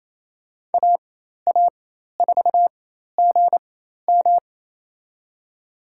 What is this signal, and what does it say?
Morse code, heard as a keyed tone, about 28 wpm, spaced as for 13 wpm, 700 Hz. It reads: AA4GM
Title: USA Call Signs - 5 Characters - 4x Character Spacing - 28wpm